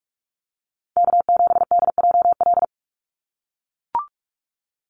Morse code, heard as a keyed tone, about 44 wpm, 700 Hz. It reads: K7DJL
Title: NA SOTA Chaser Callsigns 45wpm